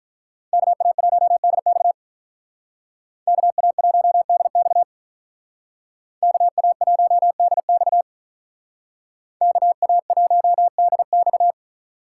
Morse code, heard as a keyed tone, about 45 wpm, 700 Hz. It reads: KA1DX KA1DX KA1DX KA1DX